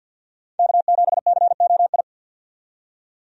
Morse code, heard as a keyed tone, about 49 wpm, 700 Hz. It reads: K7COI
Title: USA Call Signs - 5 Characters - 50wpm